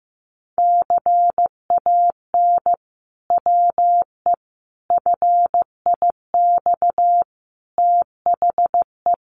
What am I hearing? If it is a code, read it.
Morse code, heard as a keyed tone, about 15 wpm, 700 Hz. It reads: CAN WE FIX THE
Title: Sentences from Top 1400 Words 15wpm